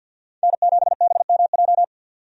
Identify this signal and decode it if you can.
Morse code, heard as a keyed tone, about 52 wpm, 700 Hz. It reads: N7BMJ
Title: USA Call Signs - 5 Characters - 50wpm